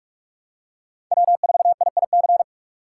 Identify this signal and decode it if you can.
Morse code, heard as a keyed tone, about 46 wpm, 700 Hz. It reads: W4IIC
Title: K1USN SST QSO Practice - 45wpm